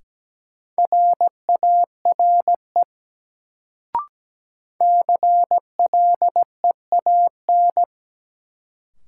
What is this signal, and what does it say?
Morse code, heard as a keyed tone, about 17 wpm, 700 Hz. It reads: RARE CLEAN
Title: Sentences from Top 1100 Words - New Word Drill - 17wpm